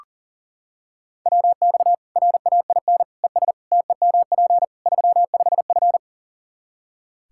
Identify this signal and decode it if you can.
Morse code, heard as a keyed tone, about 40 wpm, 700 Hz. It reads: WX RAIN ES TEMP 35F
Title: Practice QSO - Exchange 2 - Info - 40wpm